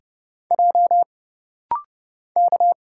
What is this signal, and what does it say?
Morse code, heard as a keyed tone, about 30 wpm, 700 Hz. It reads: J K
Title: Single Letter-Number - Rapid-Fire 30wpm